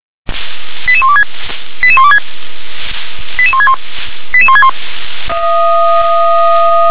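Five-tone selective call (ZVEI), heard as quick runs of five tones, tones of about 70 ms. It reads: 88126 88126 88161 88161
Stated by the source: ZVEI five-tone selective call 88665, received by SDR.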